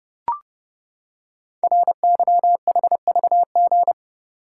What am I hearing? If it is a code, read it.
Morse code, heard as a keyed tone, about 28 wpm, 700 Hz. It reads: RYHVG
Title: Code Groups - 5 Letters - 30wpm